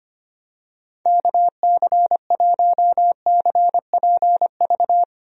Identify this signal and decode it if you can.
Morse code, heard as a keyed tone, about 25 wpm, 700 Hz. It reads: KC1CPV
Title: USA Call Signs - 6 Characters - Repeated 3x - 25wpm